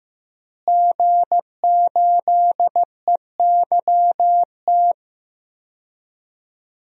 Morse code, heard as a keyed tone, about 15 wpm, 700 Hz. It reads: G8EYT
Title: DXCC-100 Call Signs - 5 Characters - Repeated 3x - 15wpm